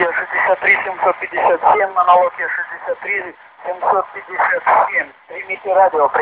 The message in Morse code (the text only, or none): none